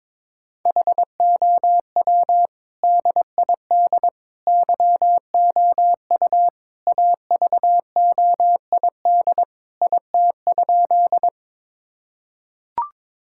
Morse code, heard as a keyed tone, about 22 wpm, 700 Hz. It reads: HOW DID YOU AVOID IT?
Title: Sentences from Top 1000 Words - Review Part 6 - 22wpm